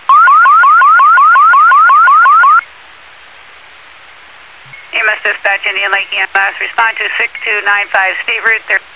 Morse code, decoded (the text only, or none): none